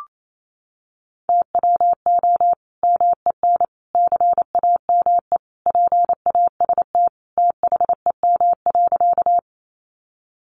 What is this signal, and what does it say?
Morse code, heard as a keyed tone, about 28 wpm, 700 Hz. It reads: TWO MEN CAME PAST THEM.
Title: Sentences from Top 600 Words 28wpm